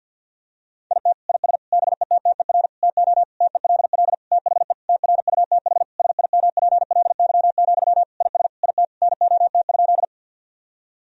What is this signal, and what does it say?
Morse code, heard as a keyed tone, about 52 wpm, 700 Hz. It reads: IT IS BETTER TO TELL THE TRUTH SIMPLY, IS IT NOT?
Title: Sentences from Top 1000 Words - Review Part 4 - 50wpm